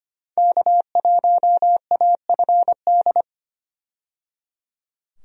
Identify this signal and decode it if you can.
Morse code, heard as a keyed tone, about 25 wpm, 700 Hz. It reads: K1AFD
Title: USA Call Signs - 5 Characters - 25wpm